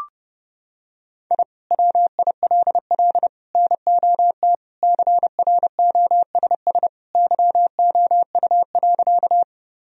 Morse code, heard as a keyed tone, about 30 wpm, 700 Hz. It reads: I WILL NOT CROSS YOU.
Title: Sentences from Top 1000 Words - Review Part 1 - 30wpm